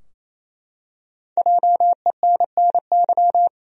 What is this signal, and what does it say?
Morse code, heard as a keyed tone, about 28 wpm, 700 Hz. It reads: JENNY
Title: USA Common Names - Max 5 Letters - 28wpm